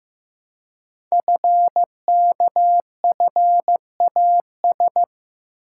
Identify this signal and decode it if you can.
Morse code, heard as a keyed tone, about 15 wpm, 700 Hz. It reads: FKFAS
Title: Code Groups - 5 Letter:Numbers - 15wpm